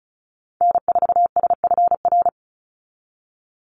Morse code, heard as a keyed tone, about 34 wpm, 700 Hz. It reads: N4SFR
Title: NA SOTA Activator Callsigns 35wpm